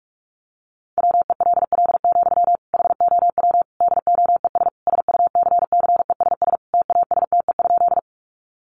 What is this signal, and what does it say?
Morse code, heard as a keyed tone, about 44 wpm, 700 Hz. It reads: WELL, HOW DOES SUCCESS TASTE?
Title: Sentences from Top 1100 Words 45wpm